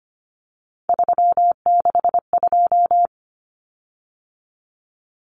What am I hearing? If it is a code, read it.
Morse code, heard as a keyed tone, about 25 wpm, 700 Hz. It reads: 362
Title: Common Cut Numbers - 3-Digit 25wpm